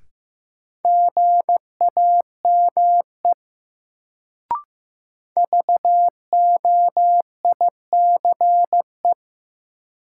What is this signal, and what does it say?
Morse code, heard as a keyed tone, about 15 wpm, 700 Hz. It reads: GAME VOICE
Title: Sentences from Top 500 Words - New Word Drill - 15wpm